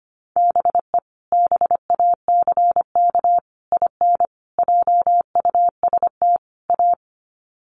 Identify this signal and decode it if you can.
Morse code, heard as a keyed tone, about 25 wpm, 700 Hz. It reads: BE BACK IN JUST A
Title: Sets of 5 Words Encore 25wpm